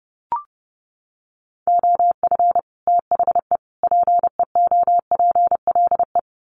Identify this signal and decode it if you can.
Morse code, heard as a keyed tone, about 30 wpm, 700 Hz. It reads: OF THE PEOPLE
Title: Sets of 3 Words 30wpm